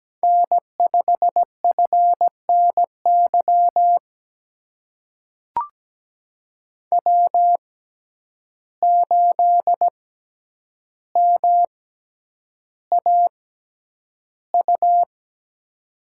Morse code, heard as a keyed tone, about 17 wpm, spaced as for 9 wpm, 700 Hz. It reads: N5FNY W8MAU
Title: USA Call Signs - 5 Characters - 6x Character Spacing - 17wpm